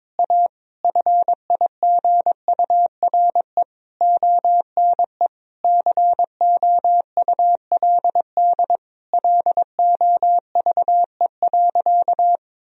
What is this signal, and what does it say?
Morse code, heard as a keyed tone, about 22 wpm, 700 Hz. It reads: A FIGURE ONE COULD LOVE.
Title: Sentences from Top 1000 Words - Review Part 2 - 22wpm